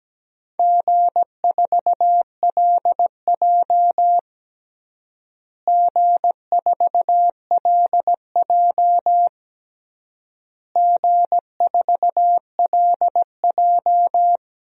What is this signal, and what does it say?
Morse code, heard as a keyed tone, about 17 wpm, 700 Hz. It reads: G4LJ G4LJ G4LJ
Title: DXCC-100 Call Signs - 4 Characters - Repeated 3x - 17wpm